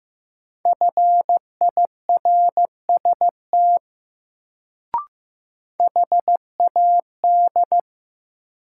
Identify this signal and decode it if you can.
Morse code, heard as a keyed tone, about 15 wpm, 700 Hz. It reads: FIRST HAD
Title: Sets of 2 Words - New Word Drill - 15wpm